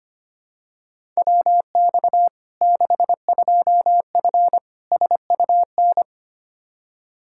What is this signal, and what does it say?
Morse code, heard as a keyed tone, about 25 wpm, 700 Hz. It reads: WX 62F SUN
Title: Practice QSO - Exchange 2 - Info - 25wpm